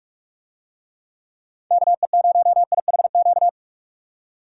Morse code, heard as a keyed tone, about 44 wpm, 700 Hz. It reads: KE0IHQ